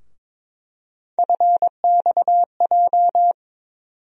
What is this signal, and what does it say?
Morse code, heard as a keyed tone, about 22 wpm, 700 Hz. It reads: FXJ
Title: USA Call Sign Postfixes - 3 Letters 22wpm